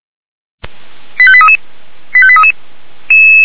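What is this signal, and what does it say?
Five-tone selective call (ZVEI), heard as quick runs of five tones, tones of about 70 ms. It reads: 85833 85833
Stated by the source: ZVEI five-tone call received on SDR.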